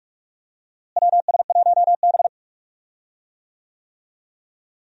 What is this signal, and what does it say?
Morse code, heard as a keyed tone, about 47 wpm, 700 Hz. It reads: WS1B
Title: USA Call Signs - 4 Characters - Repeated 2x - 45wpm